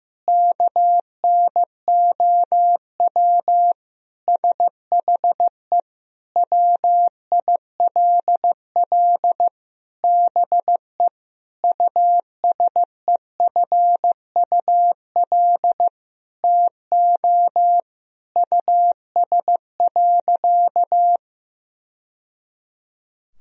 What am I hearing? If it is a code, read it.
Morse code, heard as a keyed tone, about 16 wpm, 700 Hz. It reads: KNOW SHE WILL BE USEFUL TO US.
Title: Sentences from Top 1000 Words - Review Part 4 - 15wpm